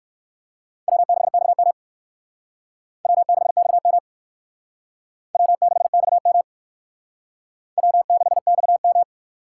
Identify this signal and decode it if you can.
Morse code, heard as a keyed tone, about 56 wpm, 700 Hz. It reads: W6XM W6XM W6XM W6XM
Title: Speed Racing - USA 1x2 Call Signs 45wpm